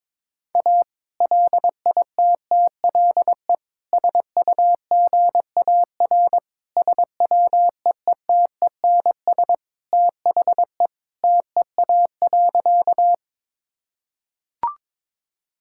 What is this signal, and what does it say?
Morse code, heard as a keyed tone, about 22 wpm, 700 Hz. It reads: A LITTLE SUGAR SWEETENS THE TEA.
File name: practice / 22wpm / Sentences with 2-Word Phrases - 22wpm